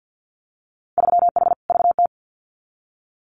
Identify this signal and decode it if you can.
Morse code, heard as a keyed tone, about 50 wpm, 700 Hz. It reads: 3H VT
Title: Winter Field Day Exchanges Encore 50wpm